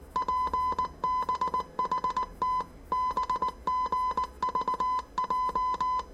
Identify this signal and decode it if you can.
Morse code, heard as a keyed tone, about 19 wpm, 1010 Hz. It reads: PBHT BGVJ
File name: 业余无线电 / ●未分类 / 字码80